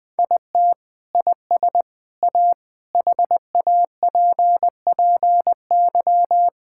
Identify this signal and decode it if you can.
Morse code, heard as a keyed tone, about 20 wpm, 700 Hz. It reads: IT IS A HAPPY
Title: Sentences from Top 1000 Words - Review Part 6 - 20wpm